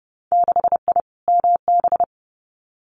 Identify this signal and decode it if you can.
Morse code, heard as a keyed tone, about 31 wpm, 700 Hz. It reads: 6I MB
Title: Winter Field Day Exchanges Encore 30wpm